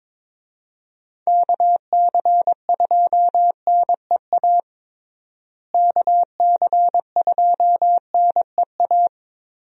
Morse code, heard as a keyed tone, about 22 wpm, 700 Hz. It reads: KC2NEA KC2NEA
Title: USA Call Signs - 6 Characters - Repeated 2x - 22wpm